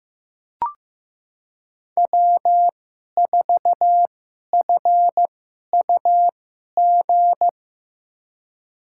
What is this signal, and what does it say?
Morse code, heard as a keyed tone, about 16 wpm, spaced as for 11 wpm, 700 Hz. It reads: W4FUG
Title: USA Call Signs - 5 Characters - 2x Character Spacing - 15wpm